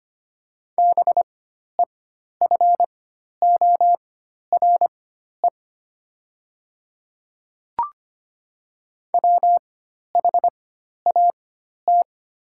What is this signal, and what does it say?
Morse code, heard as a keyed tone, about 25 wpm, spaced as for 12 wpm, 700 Hz. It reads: BEFORE WHAT
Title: 4x Character Spacing - Top 300 Words - 25wpm